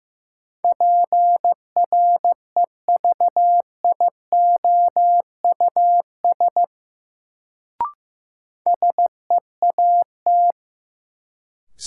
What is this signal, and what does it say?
Morse code, heard as a keyed tone, about 15 wpm, 700 Hz. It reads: PREVIOUS SEAT
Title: Sentences from Top 1500 Words - New Word Drill - 15wpm